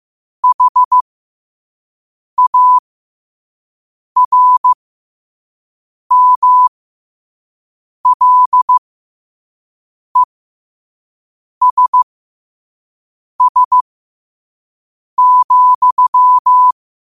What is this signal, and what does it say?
Morse code, heard as a keyed tone, about 15 wpm, spaced as for 5 wpm, 1000 Hz. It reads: HARMLESS,
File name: qotdfiles / 5wpm